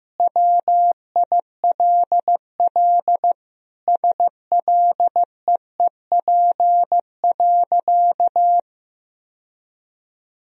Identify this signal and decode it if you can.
Morse code, heard as a keyed tone, about 15 wpm, 700 Hz. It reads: WILL SLEEP.
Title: Sentences from Top 1000 Words - Review Part 2 - 15wpm